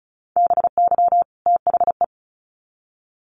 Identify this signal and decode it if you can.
Morse code, heard as a keyed tone, about 34 wpm, 700 Hz. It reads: BY THE
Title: Sets of 2 Words 35wpm